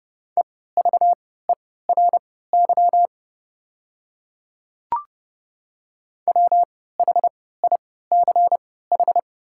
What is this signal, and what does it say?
Morse code, heard as a keyed tone, about 29 wpm, spaced as for 17 wpm, 690 Hz. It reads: EVERY WHICH
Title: 3x Character Spacing - Top 200 Words - 30wpm